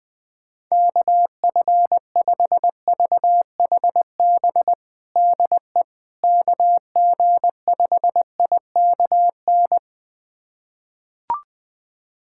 Morse code, heard as a keyed tone, about 20 wpm, 700 Hz. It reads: KF5VHB DE KG5IKN
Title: USA Callsign DE Callsign - 6 Characters - 20wpm